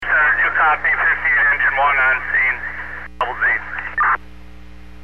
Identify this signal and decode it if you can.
Emergency Alert System (SAME) tones, heard as harsh buzzing bursts, with speech. It is heard from 0.8 to 3.9 s.